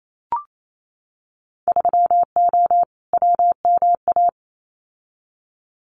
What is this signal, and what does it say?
Morse code, heard as a keyed tone, about 28 wpm, 700 Hz. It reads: 3O WMA
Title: Winter Field Day Exchanges Encore 28wpm